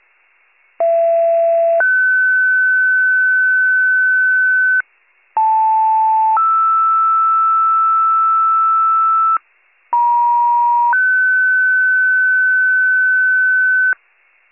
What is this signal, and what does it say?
Two-tone page, heard as a one-second tone, then a three-second tone, about 670 then 1500 Hz, about 880 then 1320 Hz, about 950 then 1500 Hz. It is heard from 0.8 to 13.9 s.